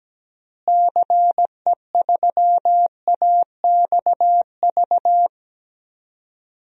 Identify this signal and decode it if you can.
Morse code, heard as a keyed tone, about 17 wpm, 700 Hz. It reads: CE3AXV
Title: DXCC-100 Call Signs - 6 Characters - Repeated 2x - 17wpm